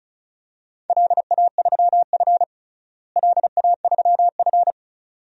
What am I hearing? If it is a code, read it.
Morse code, heard as a keyed tone, about 36 wpm, 700 Hz. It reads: LA3F LA3F
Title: DXCC-100 Call Signs - 4 Characters - Repeated 2x - 35wpm